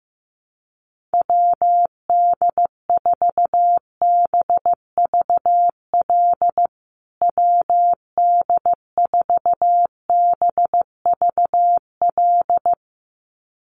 Morse code, heard as a keyed tone, about 15 wpm, 700 Hz. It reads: WD4BVL WD4BVL
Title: USA Call Signs Repeated 2x 15wpm